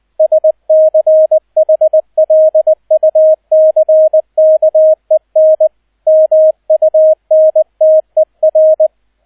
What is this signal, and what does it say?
Morse code, heard as a keyed tone, about 20 wpm, 630 Hz, with radio noise. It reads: SCHLUCKEN MUNTER
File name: Erster Streich 20wpm